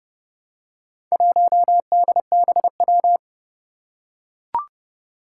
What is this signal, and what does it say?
Morse code, heard as a keyed tone, about 28 wpm, 700 Hz. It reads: 1DBW
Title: Four Challenging Letter-Number Combinations 30wpm